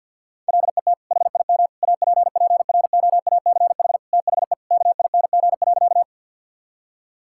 Morse code, heard as a keyed tone, about 50 wpm, 700 Hz. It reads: LET HIM APPROACH THE KING.
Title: Sentences from Top 1000 Words - Review Part 6 - 50wpm